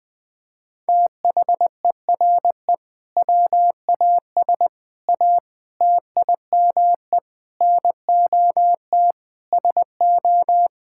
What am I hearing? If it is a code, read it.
Morse code, heard as a keyed tone, about 20 wpm, 700 Hz. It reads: THERE WAS A TIME NOT SO
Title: Sentences with 8-Word Phrases - Spoken Beginning - 20wpm